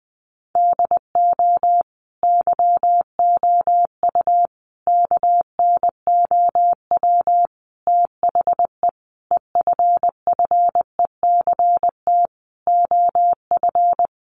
Morse code, heard as a keyed tone, about 20 wpm, 700 Hz. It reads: DO YOU KNOW THE EFFECT OF
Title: Sentences from Top 1400 Words 20wpm